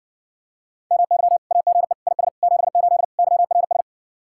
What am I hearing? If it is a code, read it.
Morse code, heard as a keyed tone, about 60 wpm, 700 Hz. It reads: MY AGE IS 78 YRS